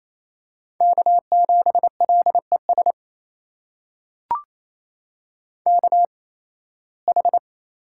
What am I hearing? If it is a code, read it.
Morse code, heard as a keyed tone, about 28 wpm, spaced as for 18 wpm, 700 Hz. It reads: K7LES KH
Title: USA Call Signs - 5 Characters - 8x Character Spacing - 28wpm